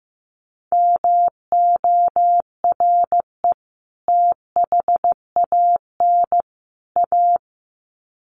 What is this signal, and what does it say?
Morse code, heard as a keyed tone, about 15 wpm, 700 Hz. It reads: MORE THAN A
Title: Sets of 3 Words 15wpm